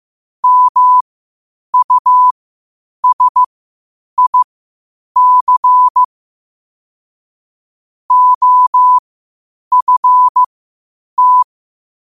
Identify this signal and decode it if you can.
Morse code, heard as a keyed tone, about 15 wpm, spaced as for 8 wpm, 1000 Hz. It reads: MUSIC OFT
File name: 8wpm